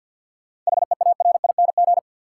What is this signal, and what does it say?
Morse code, heard as a keyed tone, about 53 wpm, 700 Hz. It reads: HEARING
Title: Seven Letter Words - 50wpm